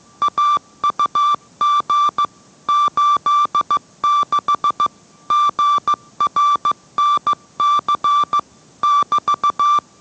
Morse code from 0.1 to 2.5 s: AUG